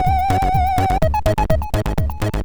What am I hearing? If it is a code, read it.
There is no code.